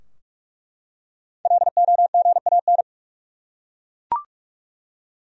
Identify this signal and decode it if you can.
Morse code, heard as a keyed tone, about 46 wpm, 700 Hz. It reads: LOGAN
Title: USA Common Names - Max 5 Letters - 45wpm